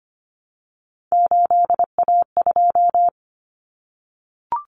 Morse code, heard as a keyed tone, about 25 wpm, 700 Hz. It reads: 8A2
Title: Common Cut Numbers - 3-Digit 25wpm